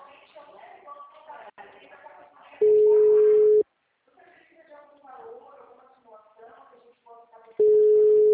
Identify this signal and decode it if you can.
Ringback tone (Europe, 425 Hz), with speech.
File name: out- -2007- - - .1509